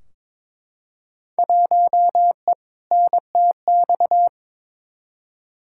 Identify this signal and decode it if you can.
Morse code, heard as a keyed tone, about 22 wpm, 700 Hz. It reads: 1E NTX
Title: Field Day Exchanges - 22wpm